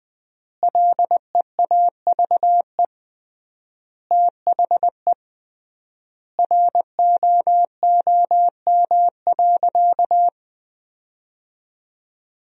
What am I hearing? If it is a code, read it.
Morse code, heard as a keyed tone, about 21 wpm, 700 Hz. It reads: LEAVE THE ROOM.
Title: 3x Word-Spacing - Sentences from Top 400 Words - 20wpm